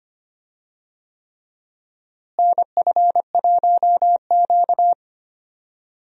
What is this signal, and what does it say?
Morse code, heard as a keyed tone, about 25 wpm, 700 Hz. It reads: NF1Q
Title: USA Call Signs - 4 Characters - Repeated 3x - 25wpm